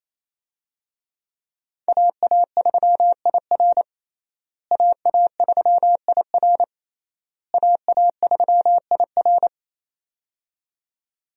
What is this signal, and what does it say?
Morse code, heard as a keyed tone, about 28 wpm, 700 Hz. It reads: AA3IR AA3IR AA3IR
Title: USA Call Signs - 5 Characters - Repeated 3x - 28wpm